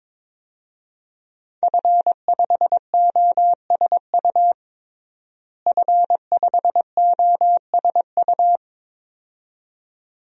Morse code, heard as a keyed tone, about 22 wpm, 700 Hz. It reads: F5OSU F5OSU
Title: DXCC-100 Call Signs - 5 Characters - Repeated 2x - 22wpm